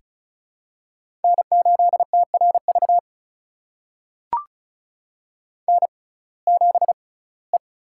Morse code, heard as a keyed tone, about 36 wpm, spaced as for 21 wpm, 700 Hz. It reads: N8TRV N7E
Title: USA Call Signs - 5 Characters - 6x Character Spacing - 35wpm